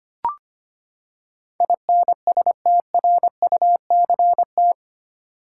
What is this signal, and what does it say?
Morse code, heard as a keyed tone, about 25 wpm, 700 Hz. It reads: INSTRUCT
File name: Eight Letter Words - 25wpm